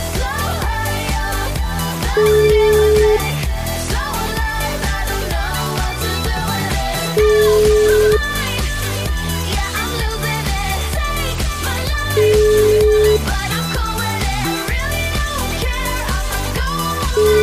Ringback tone (Europe, 425 Hz), with noise.